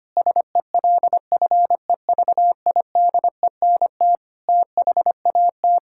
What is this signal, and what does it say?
Morse code, heard as a keyed tone, about 25 wpm, 700 Hz. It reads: SELFEVIDENT THAT
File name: Sets of 9 Words - 25wpm